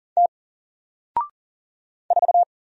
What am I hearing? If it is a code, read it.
Morse code, heard as a keyed tone, about 40 wpm, 700 Hz. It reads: T 4
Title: Single Character (Letter:Number:Punctuation:Prosign) 40wpm